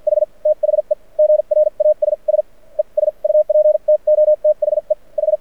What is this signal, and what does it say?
Morse code, heard as a keyed tone, about 56 wpm, spaced as for 39 wpm, 600 Hz. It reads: H THE GRASSES UP TO THE H
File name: archived / CRSnet